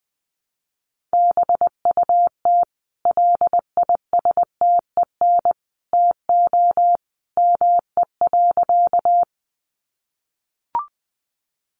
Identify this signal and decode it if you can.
Morse code, heard as a keyed tone, about 20 wpm, 700 Hz. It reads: BUT LISTEN TO ME.
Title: Sentences from Top 400 Words 20wpm